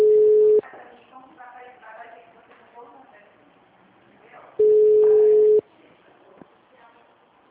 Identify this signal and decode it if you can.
Ringback tone (Europe, 425 Hz), with speech.
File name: out- -2013- - - .27844